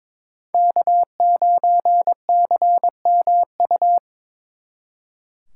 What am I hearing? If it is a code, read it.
Morse code, heard as a keyed tone, about 26 wpm, 700 Hz. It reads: K9CMU
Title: Speed Racing - USA Call Signs 22wpm